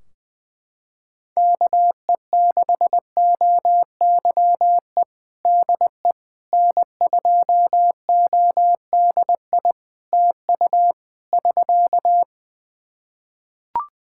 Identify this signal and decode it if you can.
Morse code, heard as a keyed tone, about 19 wpm, 700 Hz. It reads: KE6OYE DE N2ODI TU <SK>
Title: K1USN SST QSO Practice 20wpm